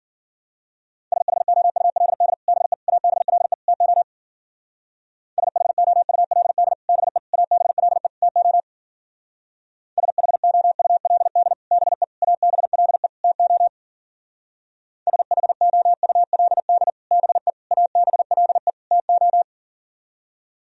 Morse code, heard as a keyed tone, about 50 wpm, 700 Hz. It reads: SHOULD BE ABLE TO SHOULD BE ABLE TO SHOULD BE ABLE TO SHOULD BE ABLE TO
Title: Speed-Racing - Sets of 4 Words 40wpm